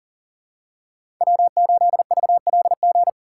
Morse code, heard as a keyed tone, about 40 wpm, 700 Hz. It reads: W8VLG
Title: USA Call Signs - 3x Character Spacing - 40wpm